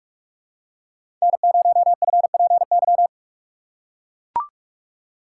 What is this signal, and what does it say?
Morse code, heard as a keyed tone, about 44 wpm, 700 Hz. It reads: N0FPY
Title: USA Call Signs - 2x Character Spacing - 45wpm